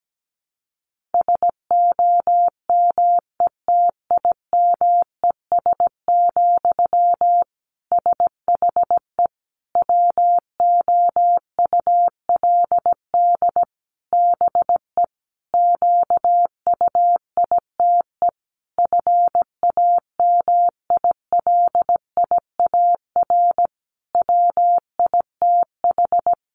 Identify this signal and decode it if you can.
Morse code, heard as a keyed tone, about 17 wpm, 700 Hz. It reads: SOMETIMES, SHE WOULD BE QUITE FAMILIAR WITH
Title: Sentences from Top 1000 Words 17wpm